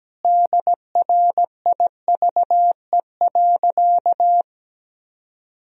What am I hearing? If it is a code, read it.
Morse code, heard as a keyed tone, about 17 wpm, 700 Hz. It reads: DRIVE.
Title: Sentences from Top 1000 Words - Review Part 2 - 17wpm